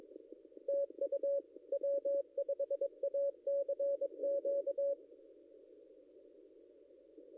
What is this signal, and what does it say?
Morse code, heard as a keyed tone, about 21 wpm, 560 Hz. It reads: TU W5ACQ
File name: contests / ss-cw-k5tr-op-n5rz -0800